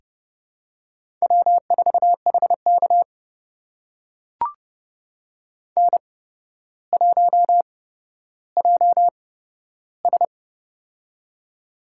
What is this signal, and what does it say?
Morse code, heard as a keyed tone, about 31 wpm, spaced as for 13 wpm, 700 Hz. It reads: W4HK N1JS
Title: USA Call Signs - 8x Character Spacing - 30wpm